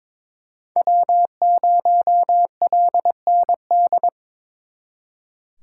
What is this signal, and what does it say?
Morse code, heard as a keyed tone, about 25 wpm, 700 Hz. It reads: W0LND
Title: Speed Racing - USA 1x3 Call Signs 22wpm